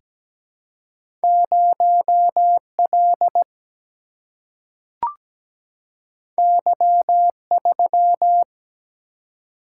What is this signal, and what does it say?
Morse code, heard as a keyed tone, about 17 wpm, 700 Hz. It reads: 0L Y3
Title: Single-Letter Single-Number 17wpm